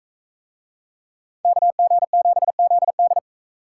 Morse code, heard as a keyed tone, about 44 wpm, 700 Hz. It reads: KG7ZD